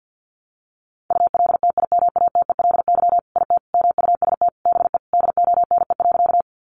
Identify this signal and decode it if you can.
Morse code, heard as a keyed tone, about 50 wpm, 700 Hz. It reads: ULTIMATELY IT MUST BE DONE.
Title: Sentences from Top 1200 Words 50wpm